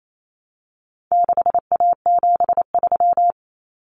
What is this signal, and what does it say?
Morse code, heard as a keyed tone, about 28 wpm, 700 Hz. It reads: 6A73
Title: Common Cut Numbers - 4-Digit 28wpm